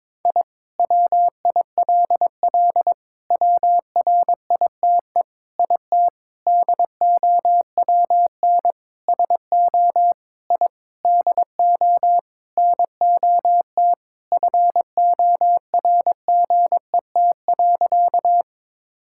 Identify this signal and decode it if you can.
Morse code, heard as a keyed tone, about 22 wpm, 700 Hz. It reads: I WILL WRITE IT DOWN SO I DO NOT FORGET.
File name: practice / 22wpm / Sentences with 4-Word Phrases - 22wpm